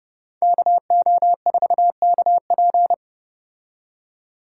Morse code, heard as a keyed tone, about 29 wpm, 700 Hz. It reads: KO4KP